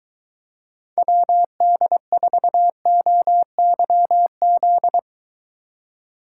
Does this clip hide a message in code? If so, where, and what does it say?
Yes: Morse code from 1.0 to 5.0 s, heard as a keyed tone, about 21 wpm, 700 Hz. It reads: WD4OYZ